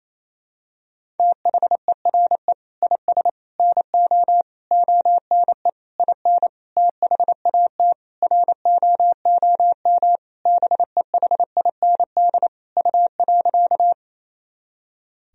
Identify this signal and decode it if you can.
Morse code, heard as a keyed tone, about 28 wpm, 700 Hz. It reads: THERE IS NO ONE IN THAT ROOM BEHIND U.
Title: Sentences from Top 500 Words and Abbreviations 28wpm